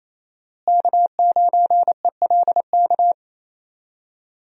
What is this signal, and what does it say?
Morse code, heard as a keyed tone, about 28 wpm, 700 Hz. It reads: K9ELK
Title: USA Call Signs - 5 Characters - Repeated 2x - 28wpm